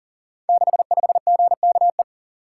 Morse code, heard as a keyed tone, about 40 wpm, 700 Hz. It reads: 65GKE
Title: Code Groups - 5 Letter:Numbers - 40wpm